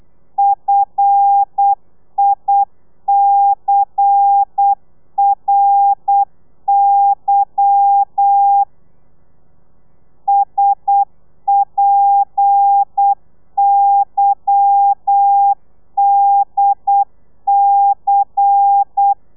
Morse code, heard as a keyed tone, about 8 wpm, 800 Hz. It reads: FICRY SPYDC
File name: telegrafia / i2tfj / Lez10 35